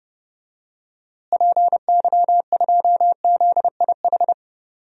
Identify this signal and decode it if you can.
Morse code, heard as a keyed tone, about 30 wpm, 700 Hz. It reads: PY2ZIH
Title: DXCC-100 Call Signs - 6 Characters - Repeated 3x - 30wpm